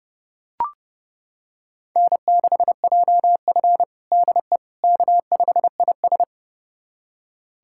Morse code, heard as a keyed tone, about 30 wpm, 700 Hz. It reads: N6JF DE K5IS
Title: USA Callsign DE Callsign - 4 Characters - 30wpm